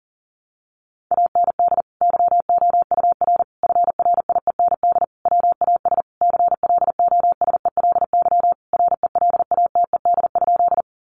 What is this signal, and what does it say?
Morse code, heard as a keyed tone, about 40 wpm, 700 Hz. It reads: AND YOUR FRIEND WAS CLOSELY RELATED?
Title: Sentences from Top 1400 Words 40wpm